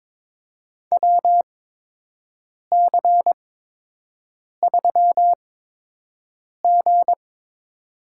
Morse code, heard as a keyed tone, about 22 wpm, spaced as for 6 wpm, 700 Hz. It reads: WC3G
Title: USA Call Signs - 4 Characters - 8x Character Spacing - 22wpm